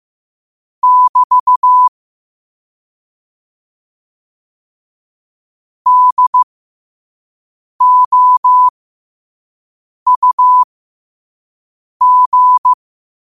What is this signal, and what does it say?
Morse code, heard as a keyed tone, about 15 wpm, spaced as for 5 wpm, 1000 Hz. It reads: <BT> DOUG
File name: qotdfiles / 5wpm